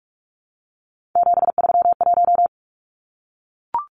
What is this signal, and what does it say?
Morse code, heard as a keyed tone, about 44 wpm, 700 Hz. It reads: 731
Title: Numbers - 3-Digit 45wpm